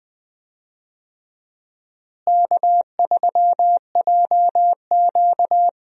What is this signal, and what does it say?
Morse code, heard as a keyed tone, about 20 wpm, 700 Hz. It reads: K3JQ
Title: USA Call Signs - 4 Characters - Repeated 3x - 20wpm